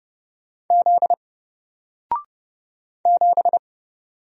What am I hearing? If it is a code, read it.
Morse code, heard as a keyed tone, about 30 wpm, 700 Hz. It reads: Z 7
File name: Single-Letter-Number-30wpm